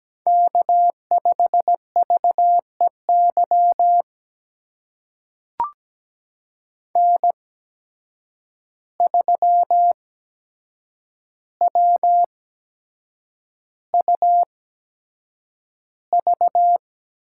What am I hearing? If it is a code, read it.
Morse code, heard as a keyed tone, about 17 wpm, spaced as for 7 wpm, 700 Hz. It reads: K5VEY N3WUV
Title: USA Call Signs - 5 Characters - 8x Character Spacing - 17wpm